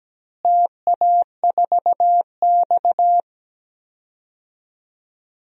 Morse code, heard as a keyed tone, about 17 wpm, 700 Hz. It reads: TA4X
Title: DXCC-100 Call Signs - 4 Characters - Repeated 2x - 17wpm